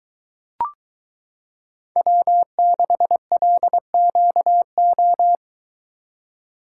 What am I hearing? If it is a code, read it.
Morse code, heard as a keyed tone, about 22 wpm, 700 Hz. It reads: W6LQO